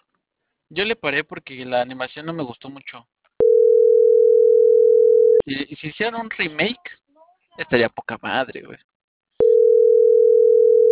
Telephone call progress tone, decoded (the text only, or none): ringback tone